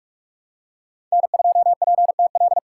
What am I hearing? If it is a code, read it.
Morse code, heard as a keyed tone, about 46 wpm, 700 Hz. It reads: N2PTL
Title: USA Call Signs - 5 Characters - 2x Character Spacing - 45wpm